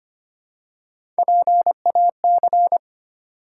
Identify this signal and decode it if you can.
Morse code, heard as a keyed tone, about 25 wpm, 700 Hz. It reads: PAC
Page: morse-code-ninja-practice-25wpm